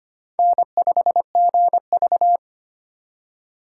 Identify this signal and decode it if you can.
Morse code, heard as a keyed tone, about 25 wpm, 700 Hz. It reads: N5GV
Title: USA Call Signs - 4 Characters - 3x Character Spacing - 25wpm